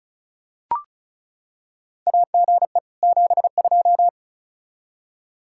Morse code, heard as a keyed tone, about 36 wpm, 700 Hz. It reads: AGE 72
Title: Practice QSO - Exchange 3 - More Info - 35wpm